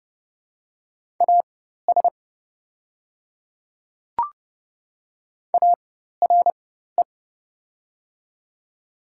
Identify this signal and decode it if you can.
Morse code, heard as a keyed tone, about 30 wpm, spaced as for 14 wpm, 690 Hz. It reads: AS ARE
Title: 4x Character Spacing - Top 100 Words - 30wpm